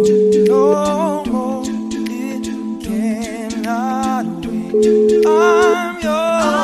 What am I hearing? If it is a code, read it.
Ringback tone (Europe, 425 Hz), with noise.